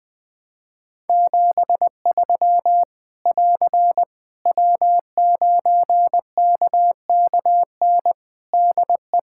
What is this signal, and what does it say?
Morse code, heard as a keyed tone, about 20 wpm, 700 Hz. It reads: 73 <AR> W9KKN DE
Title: Practice QSO - Exchange 4 - Ending - 20wpm